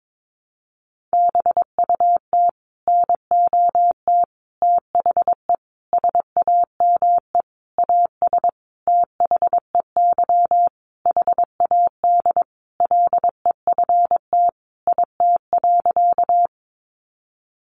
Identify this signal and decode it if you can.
Morse code, heard as a keyed tone, about 22 wpm, 700 Hz. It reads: BUT NOT THE SAME AS THEY HAD LEFT IT.
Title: A Sound Of Thunder 22wpm